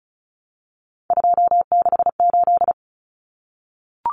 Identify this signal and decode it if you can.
Morse code, heard as a keyed tone, about 34 wpm, 700 Hz. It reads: 268
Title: Common Cut Numbers - 3-Digit 35wpm